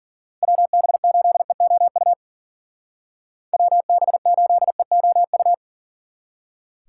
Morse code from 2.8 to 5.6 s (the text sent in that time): WB8EOU